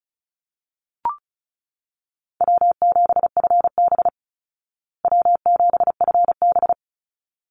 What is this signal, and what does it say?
Morse code, heard as a keyed tone, about 34 wpm, 700 Hz. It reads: W7FB W7FB
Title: Easy Call Signs Repeated 2x - 35wpm